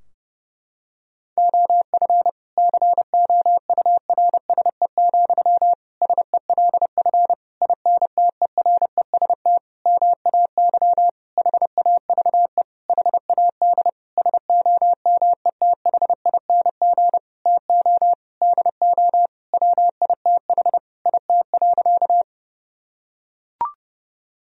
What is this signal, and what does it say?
Morse code, heard as a keyed tone, about 30 wpm, 700 Hz. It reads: OF COURSE, SELF INTEREST MAY HAVE HAD SOMETHING TO DO WITH IT.
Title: Sentences from Top 1000 Words - Review Part 2 - 30wpm